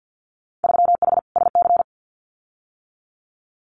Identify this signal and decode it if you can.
Morse code, heard as a keyed tone, about 48 wpm, 700 Hz. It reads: 3H SC
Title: Winter Field Day Exchanges 50wpm